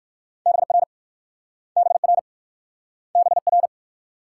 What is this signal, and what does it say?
Morse code, heard as a keyed tone, about 56 wpm, 700 Hz. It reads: BR BR BR
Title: Speed-Racing Multi-letter Phonemes - 45wpm